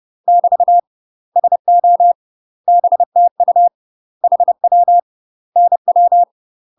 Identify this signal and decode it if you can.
Morse code, heard as a keyed tone, about 30 wpm, 700 Hz. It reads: <BT> SO BTU HW NW
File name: UK MASTER QSOs TRAINING / 30WPM / UK MASTER QSOs 0022